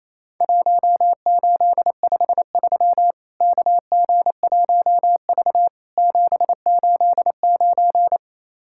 Morse code, heard as a keyed tone, about 28 wpm, 700 Hz. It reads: 1853 KG1V 789
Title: Worked All Europe DX Contest - QTC Exchange - 28wpm